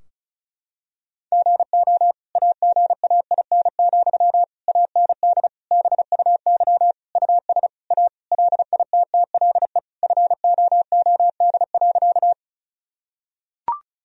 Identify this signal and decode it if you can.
Morse code, heard as a keyed tone, about 34 wpm, 700 Hz. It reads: GO AGAIN, AND BUY US A LITTLE FOOD.
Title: Sentences from Top 1000 Words - Review Part 4 - 35wpm